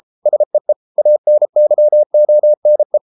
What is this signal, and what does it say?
Morse code, heard as a keyed tone, about 33 wpm, 600 Hz. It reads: SEE ANYONE